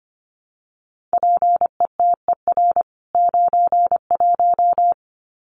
Morse code, heard as a keyed tone, about 25 wpm, 700 Hz. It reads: PETER 91
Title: CWT Exchange 25wpm